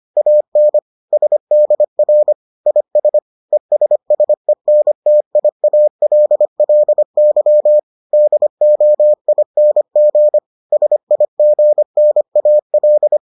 Morse code, heard as a keyed tone, about 25 wpm, 600 Hz. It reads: AN SDR IS ESSENTIALLY DOING SIGNAL